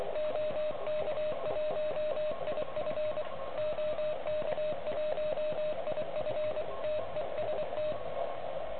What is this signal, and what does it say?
Morse code, heard as a keyed tone, about 23 wpm, 610 Hz, with radio noise. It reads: OK1IF OK1IF TEST
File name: ARCHIV / OK QRP 2021 / REC10